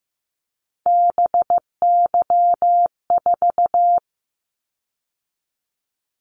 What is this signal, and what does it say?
Morse code, heard as a keyed tone, about 15 wpm, 700 Hz. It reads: BY4
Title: Callsign Prefixes for Top 100 DXCC - 15wpm